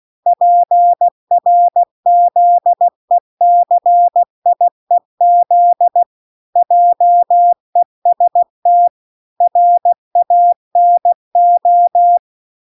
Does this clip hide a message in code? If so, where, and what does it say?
Yes: Morse code from 0.3 to 12.2 s, heard as a keyed tone, about 16 wpm, 700 Hz. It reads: PRZECIEZ JEST RANO